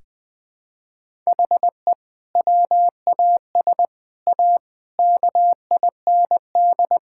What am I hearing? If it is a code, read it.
Morse code, heard as a keyed tone, about 21 wpm, 700 Hz. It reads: HE WAS A KIND
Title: Sentences with 3-Word Phrases - 20wpm